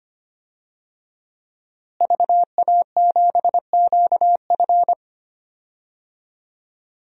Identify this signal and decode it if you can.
Morse code, heard as a keyed tone, about 25 wpm, 700 Hz. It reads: VA7QF
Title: DXCC-100 Call Signs - 5 Characters - Repeated 3x - 25wpm